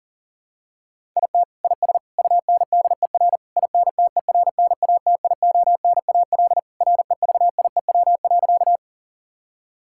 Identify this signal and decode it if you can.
Morse code, heard as a keyed tone, about 40 wpm, 700 Hz. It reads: IT IS UNDER INTERNATIONAL REVIEW.
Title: Sentences from Top 1000 Words - Review Part 6 - 40wpm